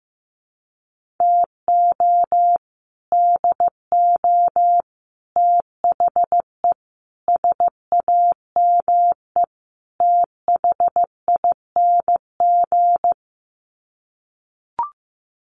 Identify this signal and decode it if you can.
Morse code, heard as a keyed tone, about 15 wpm, 700 Hz. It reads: TO DO THE SAME THING
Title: Sets of 5 Words Encore 15wpm